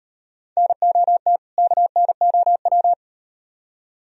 Morse code, heard as a keyed tone, about 38 wpm, 700 Hz. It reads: NOT KNOW